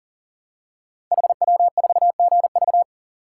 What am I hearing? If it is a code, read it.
Morse code, heard as a keyed tone, about 40 wpm, 700 Hz. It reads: HW4GV
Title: Code Groups - 5 Letter:Numbers - 40wpm